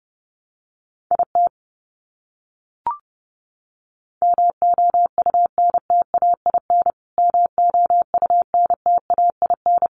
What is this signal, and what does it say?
Morse code, heard as a keyed tone, about 30 wpm, 700 Hz. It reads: IT MOUNTAIN MOUNTAIN
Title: Top 500 Words Repeated 30wpm